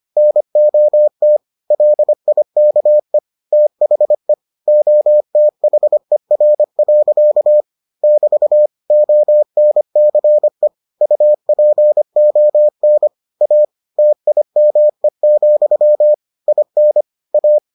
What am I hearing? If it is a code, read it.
Morse code, heard as a keyed tone, about 25 wpm, 600 Hz. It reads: NOT LIKE THE OTHER. <BT> ONCE UPON A TIME, IN A